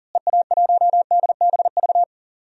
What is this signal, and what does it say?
Morse code, heard as a keyed tone, about 40 wpm, 700 Hz. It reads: EA1DBV
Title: DXCC-100 Call Signs - 6 Characters - Repeated 3x - 40wpm